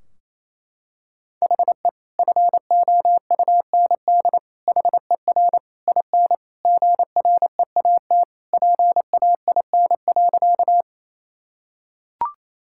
Morse code, heard as a keyed tone, about 28 wpm, 700 Hz. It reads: HE FOUND HER IN GREAT PAIN.
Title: Sentences from Top 1600 Words 28wpm